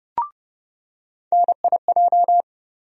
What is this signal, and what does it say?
Morse code, heard as a keyed tone, about 30 wpm, 700 Hz. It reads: NIJ
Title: USA Call Sign Postfixes - 3 Letters 30wpm